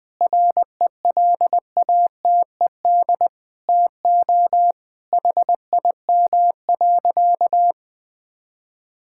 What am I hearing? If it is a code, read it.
Morse code, heard as a keyed tone, about 20 wpm, 700 Hz. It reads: RELATED TO HIM.
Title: Sentences from Top 1000 Words - Review Part 5 - 20wpm